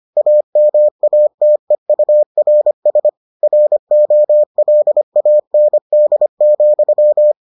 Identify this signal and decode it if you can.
Morse code, heard as a keyed tone, about 25 wpm, 600 Hz. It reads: AMATEURS ROLAND,